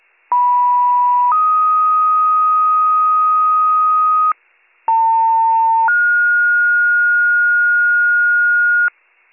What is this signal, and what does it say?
Two-tone page, heard as a one-second tone, then a three-second tone, about 980 then 1250 Hz, about 900 then 1430 Hz, with noise. It is heard from 0.3 to 8.9 s.